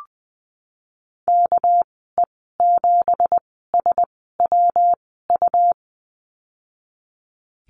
Morse code, heard as a keyed tone, about 20 wpm, spaced as for 15 wpm, 700 Hz. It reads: KE7SWU
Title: 200 Lesson 32g - Letter k - Call Signs 20wpm